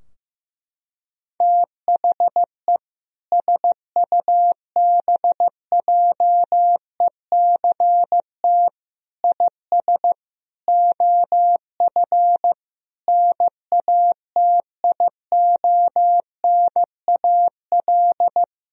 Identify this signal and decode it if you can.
Morse code, heard as a keyed tone, about 15 wpm, 700 Hz. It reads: THE SUBJECT IS OF NATIONAL
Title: Sentences from Top 1000 Words - Review Part 4 - 15wpm